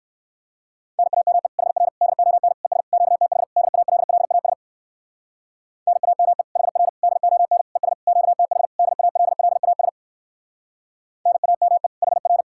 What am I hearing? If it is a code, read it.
Morse code, heard as a keyed tone, about 59 wpm, 700 Hz. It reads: NAME HR DON ES QTH DALLAS NAME HR DON ES QTH DALLAS NAME HR